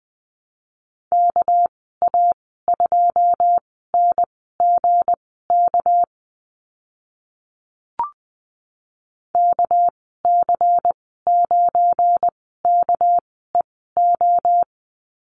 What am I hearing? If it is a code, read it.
Morse code, heard as a keyed tone, about 20 wpm, spaced as for 15 wpm, 700 Hz. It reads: KA2NGK KC9KEO
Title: 207 Lesson 33g - Letter j - Call Signs 20wpm